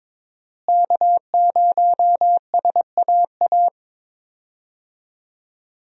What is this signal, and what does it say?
Morse code, heard as a keyed tone, about 22 wpm, 700 Hz. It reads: K0SAA
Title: USA Call Signs - 5 Characters - Repeated 3x - 22wpm